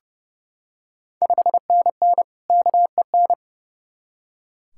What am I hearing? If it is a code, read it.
Morse code, heard as a keyed tone, about 29 wpm, 700 Hz. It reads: 5NN KEN
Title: Ontario QSO Party Exchange 30wpm